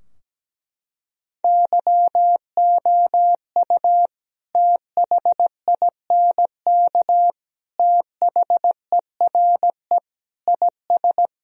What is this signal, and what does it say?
Morse code, heard as a keyed tone, about 17 wpm, 700 Hz. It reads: YOU THINK THERE IS
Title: Sentences from Top 1000 Words - Review Part 4 - 17wpm